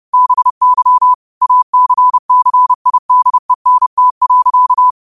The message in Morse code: BY ACCIDENT.

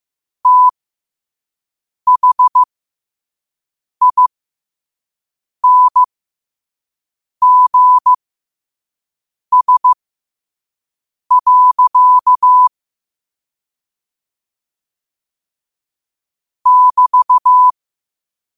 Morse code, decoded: THINGS. <BT>

